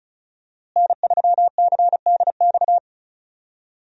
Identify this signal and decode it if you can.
Morse code, heard as a keyed tone, about 36 wpm, 700 Hz. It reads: N3CDX
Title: USA Call Signs - 5 Characters - Repeated 2x - 35wpm